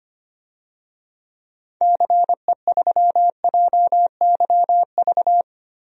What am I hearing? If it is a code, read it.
Morse code, heard as a keyed tone, about 25 wpm, 700 Hz. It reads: CE3JYV
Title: DXCC-100 Call Signs - 6 Characters - 25wpm